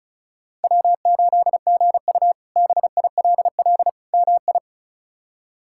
Morse code, heard as a keyed tone, about 34 wpm, 700 Hz. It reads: W8GU BILL MI
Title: K1USN SST QSO Practice - 35wpm